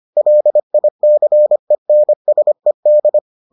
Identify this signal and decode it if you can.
Morse code, heard as a keyed tone, about 25 wpm, 600 Hz. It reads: LICENSED